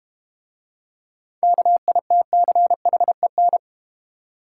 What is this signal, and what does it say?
Morse code, heard as a keyed tone, about 32 wpm, 690 Hz. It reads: KITCHEN